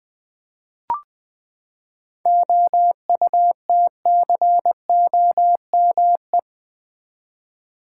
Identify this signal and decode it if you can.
Morse code, heard as a keyed tone, about 20 wpm, 700 Hz. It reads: OUTCOME
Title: Seven Letter Words - 20wpm